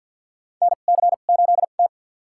Morse code, heard as a keyed tone, about 52 wpm, spaced as for 36 wpm, 700 Hz. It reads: NC8T
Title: USA Call Signs - 4 Characters - 2x Character Spacing - 50wpm